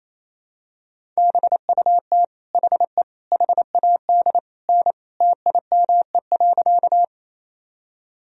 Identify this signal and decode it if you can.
Morse code, heard as a keyed tone, about 28 wpm, 700 Hz. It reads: BUT HE HAD N TIME.
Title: Sentences from Top 100 Words and Abbreviations 28wpm